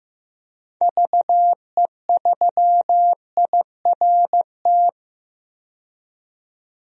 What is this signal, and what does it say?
Morse code, heard as a keyed tone, about 15 wpm, 700 Hz. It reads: VE3IRT
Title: DXCC-100 Call Signs - 6 Characters - 15wpm